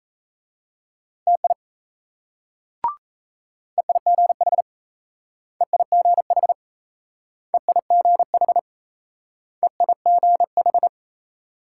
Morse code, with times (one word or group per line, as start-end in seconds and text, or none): 1.3-1.5 TI
3.8-4.6 EIGH
5.6-6.5 EIGH
7.5-8.6 EIGH
9.6-10.9 EIGH